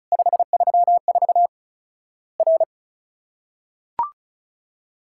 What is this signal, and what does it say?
Morse code, heard as a keyed tone, about 36 wpm, 700 Hz. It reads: 534 R
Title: Worked All Europe DX Contest - QTC Exchange - 35wpm